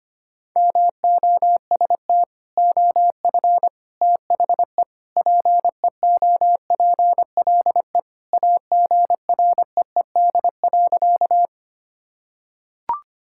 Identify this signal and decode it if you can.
Morse code, heard as a keyed tone, about 25 wpm, 700 Hz. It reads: MOST OF THE PEOPLE AGREED.
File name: practice / 25wpm / Sentences with 3-Word Phrases - 25wpm